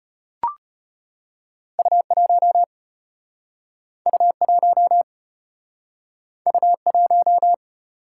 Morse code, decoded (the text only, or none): U1 U1 U1